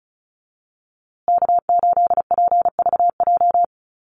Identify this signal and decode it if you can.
Morse code, heard as a keyed tone, about 34 wpm, 700 Hz. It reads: K8PVJ
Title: USA Call Signs Repeated 3x 35wpm